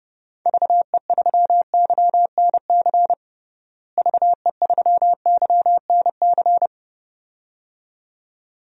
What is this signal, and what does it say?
Morse code, heard as a keyed tone, about 30 wpm, 700 Hz. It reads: VE3YNC VE3YNC
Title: DXCC-100 Call Signs - 6 Characters - Repeated 2x - 30wpm